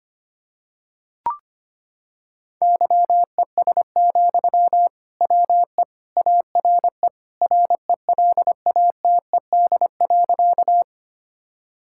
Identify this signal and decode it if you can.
Morse code, heard as a keyed tone, about 25 wpm, 700 Hz. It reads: YES, WE ARE RELATED.
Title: Sentences from Top 1000 Words - Review Part 5 - 25wpm